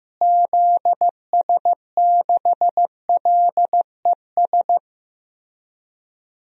Morse code, heard as a keyed tone, about 15 wpm, 700 Hz. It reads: ZS6LES
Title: DXCC-100 Call Signs - 6 Characters - Repeated 3x - 15wpm